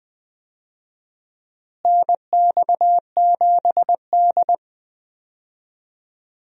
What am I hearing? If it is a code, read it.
Morse code, heard as a keyed tone, about 20 wpm, 700 Hz. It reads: NX7D
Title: USA Call Signs - 4 Characters - Repeated 2x - 20wpm